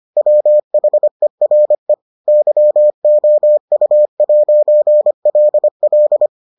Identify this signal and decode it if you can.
Morse code, heard as a keyed tone, about 25 wpm, 600 Hz. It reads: WHERE YOU'LL